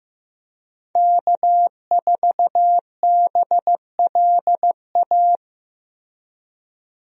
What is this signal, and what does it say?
Morse code, heard as a keyed tone, about 15 wpm, 700 Hz. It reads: K4BLA
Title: USA Call Signs - 5 Characters - Repeated 3x - 15wpm